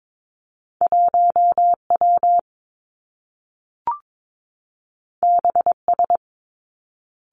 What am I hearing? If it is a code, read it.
Morse code, heard as a keyed tone, about 22 wpm, 700 Hz. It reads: 1 W B S
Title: Two Challenging Letter-Number Combinations 22wpm